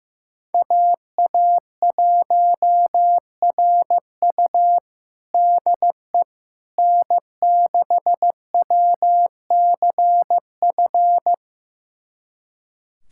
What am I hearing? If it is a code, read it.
Morse code, heard as a keyed tone, about 15 wpm, 700 Hz. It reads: AA1RU DE N6WCF
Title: USA Callsign DE Callsign - 5 Characters - 15wpm